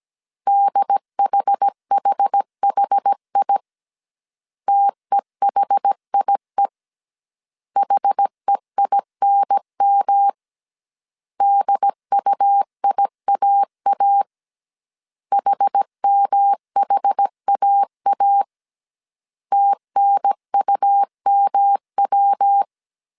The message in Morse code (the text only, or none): DHHHI TEHIE HEINM DUIAA HMHAA TNUMW